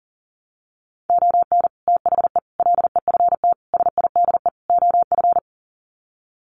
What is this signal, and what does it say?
Morse code, heard as a keyed tone, about 40 wpm, 700 Hz. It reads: ON THE LEFT SIDE OF
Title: Sets of 5 Words Encore2 40wpm